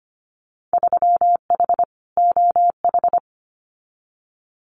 Morse code, heard as a keyed tone, about 25 wpm, 700 Hz. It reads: 3H OH
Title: Winter Field Day Exchanges Encore 25wpm